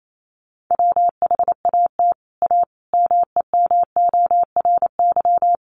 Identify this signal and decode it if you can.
Morse code, heard as a keyed tone, about 28 wpm, 700 Hz. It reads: WHAT A MEMORY
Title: Sentences from Top 1300 Words 28wpm